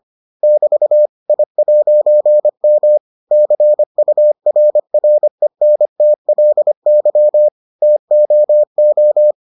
Morse code, heard as a keyed tone, about 25 wpm, 600 Hz. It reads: <BT> I'M CURRENTLY TOO